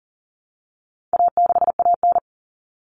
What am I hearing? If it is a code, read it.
Morse code, heard as a keyed tone, about 40 wpm, 700 Hz. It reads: A6AN
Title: Common Cut Numbers - 4-Digit 40wpm